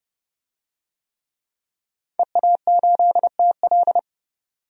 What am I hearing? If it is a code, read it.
Morse code, heard as a keyed tone, about 30 wpm, 700 Hz. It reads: EA8TL
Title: DXCC-100 Call Signs - 5 Characters - 30wpm